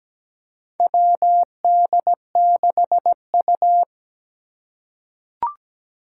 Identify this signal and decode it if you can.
Morse code, heard as a keyed tone, about 17 wpm, 700 Hz. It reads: WD6U
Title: USA Call Signs - 4 Characters - 3x Character Spacing - 17wpm